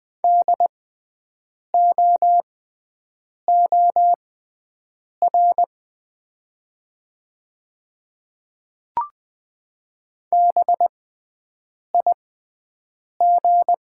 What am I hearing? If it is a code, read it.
Morse code, heard as a keyed tone, about 20 wpm, spaced as for 7 wpm, 690 Hz. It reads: DOOR BIG
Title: 6x Character Spacing - Top 300 Words - 20wpm